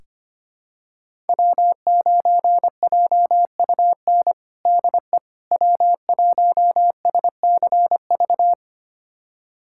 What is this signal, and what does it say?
Morse code, heard as a keyed tone, about 25 wpm, 700 Hz. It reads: W9JUN DE W1SCV
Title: USA Callsign DE Callsign - 5 Characters - 25wpm